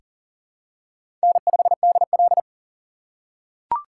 Morse code, heard as a keyed tone, about 40 wpm, 700 Hz. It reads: N5DL
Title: USA Call Signs - 2x Character Spacing - 40wpm